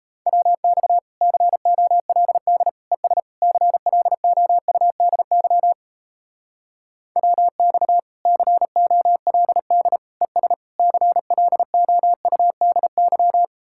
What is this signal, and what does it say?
Morse code, heard as a keyed tone, about 34 wpm, 700 Hz. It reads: WX COLD ES CLOUDY WX COLD ES CLOUDY
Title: Speed-Racing Common QSO Phrases Encore - 28wpm